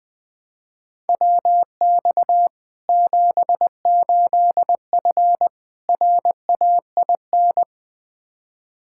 Morse code, heard as a keyed tone, about 20 wpm, 700 Hz. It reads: WX 78F RAIN
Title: K1USN SST QSO Practice 20wpm